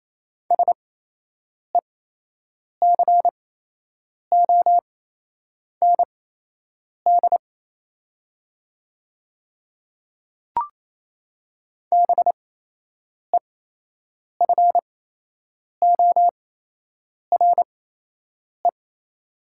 Morse code, heard as a keyed tone, about 28 wpm, spaced as for 8 wpm, 700 Hz. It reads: SECOND BEFORE
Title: 8x Character Spacing - Top 300 Words - 28wpm